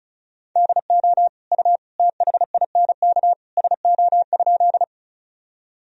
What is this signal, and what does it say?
Morse code, heard as a keyed tone, about 34 wpm, 690 Hz. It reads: DO U THINK SO?
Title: Sentences from Top 200 Words and Abbreviations 35wpm